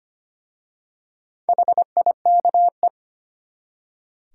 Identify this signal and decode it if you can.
Morse code, heard as a keyed tone, about 25 wpm, 700 Hz. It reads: HIKE
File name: practice / 25wpm / Easy Words with Hints - 25wpm